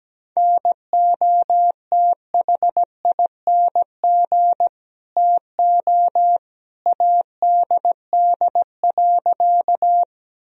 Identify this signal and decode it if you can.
Morse code, heard as a keyed tone, about 17 wpm, 700 Hz. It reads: NOTHING TO ADD.
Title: Sentences from Top 1000 Words - Review Part 1 - 17wpm